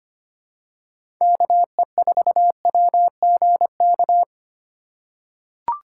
Morse code, heard as a keyed tone, about 25 wpm, 700 Hz. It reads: KE4WGK
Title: USA Call Signs - 6 Characters - 4x Character Spacing - 25wpm